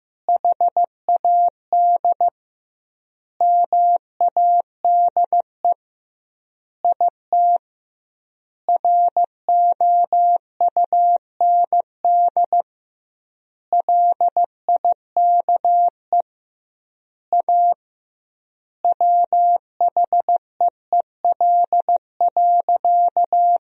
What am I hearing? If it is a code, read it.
Morse code, heard as a keyed tone, about 15 wpm, 700 Hz. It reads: HAD MADE IT ROUND LIKE A WHEEL.
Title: 2x Word-Spacing - Sentences from Top 500 Words - 15wpm